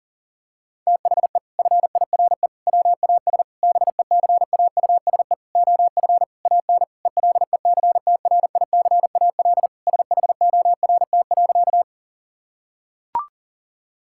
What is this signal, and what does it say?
Morse code, heard as a keyed tone, about 40 wpm, 700 Hz. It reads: THE FIRE WAS BECAUSE OF AN ELECTRICAL SHORT.
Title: Sentences from Top 1000 Words - Review Part 6 - 40wpm